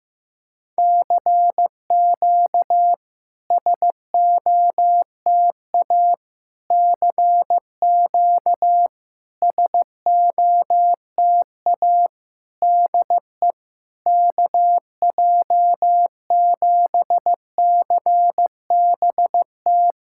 Morse code, heard as a keyed tone, about 15 wpm, 700 Hz. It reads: CQ SOTA CQ SOTA DE KJ7CBT
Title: SOTA QSO Practice 15wpm